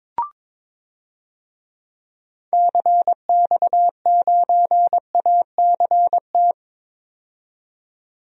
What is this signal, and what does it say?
Morse code, heard as a keyed tone, about 22 wpm, 700 Hz. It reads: CX9ACT
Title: DXCC-100 Call Signs - 6 Characters - 22wpm